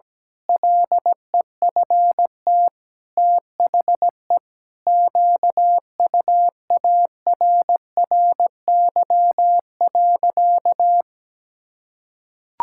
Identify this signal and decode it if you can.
Morse code, heard as a keyed tone, about 17 wpm, 700 Hz. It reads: LEFT THE QUARRY.
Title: Pangram Challenge v2 - 17wpm